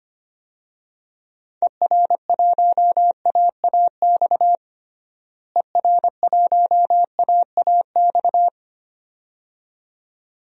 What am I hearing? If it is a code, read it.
Morse code, heard as a keyed tone, about 25 wpm, 700 Hz. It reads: ER1AAX ER1AAX
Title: DXCC-100 Call Signs - 6 Characters - Repeated 2x - 25wpm